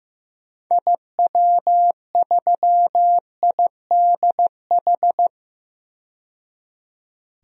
Morse code, heard as a keyed tone, about 15 wpm, 700 Hz. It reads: IW3IDH
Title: DXCC-100 Call Signs - 6 Characters - 15wpm